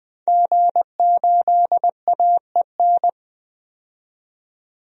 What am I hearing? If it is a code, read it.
Morse code, heard as a keyed tone, about 20 wpm, 700 Hz. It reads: G8AEN
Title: DXCC-100 Call Signs - 5 Characters - 20wpm